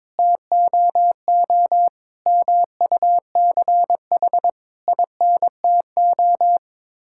Morse code, heard as a keyed tone, about 22 wpm, 700 Hz. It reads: TOO MUCH INTO
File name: Sets of 9 Words - 22wpm